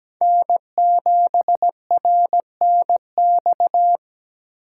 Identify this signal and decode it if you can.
Morse code, heard as a keyed tone, about 17 wpm, 700 Hz. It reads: N7RNX
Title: USA Call Signs - 8x Character Spacing - 17wpm